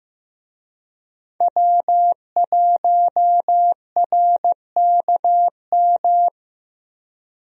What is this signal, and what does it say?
Morse code, heard as a keyed tone, about 15 wpm, 700 Hz. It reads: W1RKM
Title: USA Call Signs - 5 Characters - Repeated 2x - 15wpm